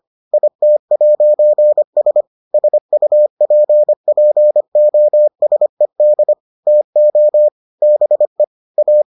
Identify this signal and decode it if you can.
Morse code, heard as a keyed tone, about 25 wpm, 600 Hz. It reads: IT'S SUPPOSED TO BE A